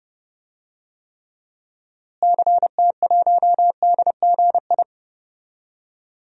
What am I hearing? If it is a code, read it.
Morse code, heard as a keyed tone, about 31 wpm, 700 Hz. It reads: CT1DGI
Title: DXCC-100 Call Signs - 6 Characters - Repeated 2x - 30wpm